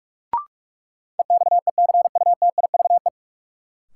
Morse code, heard as a keyed tone, about 46 wpm, 700 Hz. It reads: EXECUTIVE
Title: Sets of 5 Words Encore - New Word Drill - 45wpm